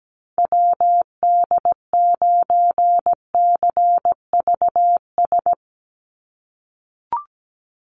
Morse code, heard as a keyed tone, about 17 wpm, 700 Hz. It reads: WD9CVS
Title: USA Call Signs 17wpm